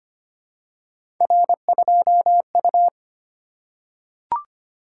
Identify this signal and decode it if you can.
Morse code, heard as a keyed tone, about 25 wpm, 700 Hz. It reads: R2U
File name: Three Letter-Number Combinations - 25wpm